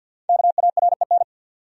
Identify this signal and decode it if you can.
Morse code, heard as a keyed tone, about 52 wpm, 700 Hz. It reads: KAREN
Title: USA Common Names - Max 5 Letters - 50wpm